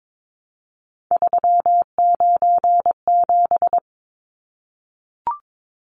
Morse code, heard as a keyed tone, about 22 wpm, 700 Hz. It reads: 397
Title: Numbers - 3-Digit 22wpm